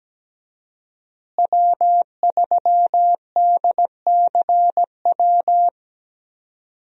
Morse code, heard as a keyed tone, about 17 wpm, 700 Hz. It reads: W3DCW
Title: USA Call Signs - 5 Characters - Repeated 3x - 17wpm